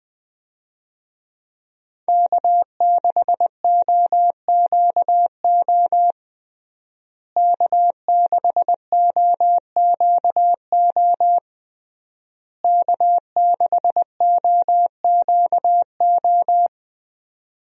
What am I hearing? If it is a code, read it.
Morse code, heard as a keyed tone, about 21 wpm, 700 Hz. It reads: K6OQO K6OQO K6OQO
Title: USA Call Signs - 5 Characters - Repeated 3x - 20wpm